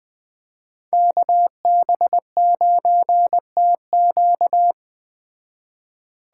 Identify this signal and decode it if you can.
Morse code, heard as a keyed tone, about 20 wpm, 700 Hz. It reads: KB9TQ
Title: Speed Racing - USA 2x2 Call Signs 17wpm